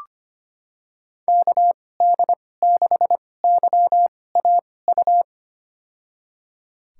Morse code, heard as a keyed tone, about 25 wpm, spaced as for 18 wpm, 700 Hz. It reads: KD6YAU
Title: USA Call Signs - 6 Characters - 2x Character Spacing - 25wpm